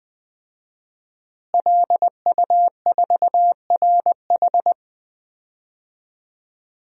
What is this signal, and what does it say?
Morse code, heard as a keyed tone, about 20 wpm, 700 Hz. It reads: LU4RH
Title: DXCC-100 Call Signs - 5 Characters - Repeated 2x - 20wpm